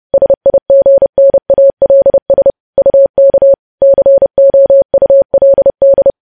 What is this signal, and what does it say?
Morse code, heard as a keyed tone, about 30 wpm, 550 Hz. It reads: SIGNALS UK COULD
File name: podcast / News headlines-1x-30x30